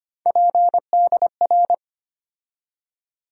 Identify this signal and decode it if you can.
Morse code, heard as a keyed tone, about 25 wpm, 700 Hz. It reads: PDR
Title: USA Call Sign Postfixes - 3 Letters 25wpm